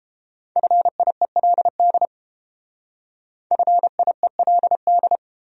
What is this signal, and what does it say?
Morse code, heard as a keyed tone, about 30 wpm, 700 Hz. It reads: FIELD FIELD